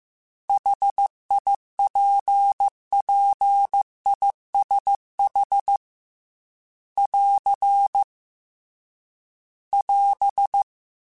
Morse code, heard as a keyed tone, about 15 wpm, 800 Hz. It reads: HIPPISH <AR> <AS>
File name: General / General-mp3s / ISH @13